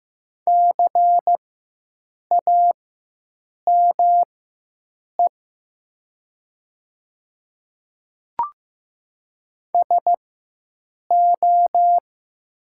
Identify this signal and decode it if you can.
Morse code, heard as a keyed tone, about 15 wpm, spaced as for 7 wpm, 700 Hz. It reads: CAME SO
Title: 4x Character Spacing - Top 200 Words - 15wpm